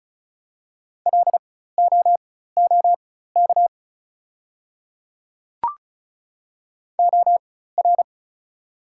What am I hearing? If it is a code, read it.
Morse code, heard as a keyed tone, about 36 wpm, spaced as for 16 wpm, 690 Hz. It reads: LOOK OR
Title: 4x Character Spacing - Top 100 Words - 35wpm